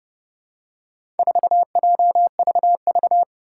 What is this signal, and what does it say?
Morse code, heard as a keyed tone, about 30 wpm, 700 Hz. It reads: 4JVV
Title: Four Challenging Letter-Number Combinations 30wpm